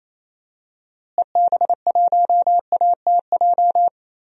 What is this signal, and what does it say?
Morse code, heard as a keyed tone, about 27 wpm, 700 Hz. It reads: EB1ATJ
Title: DXCC-100 Call Signs - 6 Characters - Repeated 2x - 28wpm